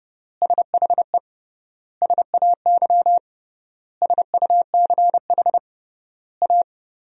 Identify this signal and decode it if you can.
Morse code, heard as a keyed tone, about 30 wpm, 700 Hz. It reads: SHE SAY SUCH A
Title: 3x Word-Spacing - Sentences from Top 200 Words - 30wpm